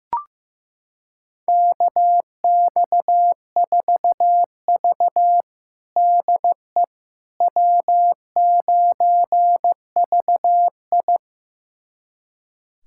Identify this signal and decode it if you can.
Morse code, heard as a keyed tone, about 16 wpm, 700 Hz. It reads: KX4V DE W9VI
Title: USA Callsign DE Callsign - 4 Characters - 15wpm